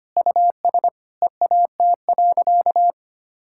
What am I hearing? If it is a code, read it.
Morse code, heard as a keyed tone, about 25 wpm, 700 Hz. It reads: US EAT.
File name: Sentences with 3-Word Phrases - 25wpm